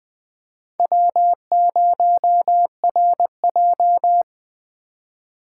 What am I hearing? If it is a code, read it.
Morse code, heard as a keyed tone, about 20 wpm, 700 Hz. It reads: W0RJ
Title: Speed Racing - USA 1x2 Call Signs 17wpm